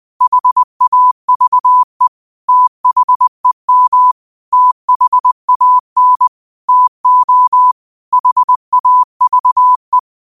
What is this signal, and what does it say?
Morse code, heard as a keyed tone, about 20 wpm, 1000 Hz. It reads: HAVE THEM THAN TO HAVE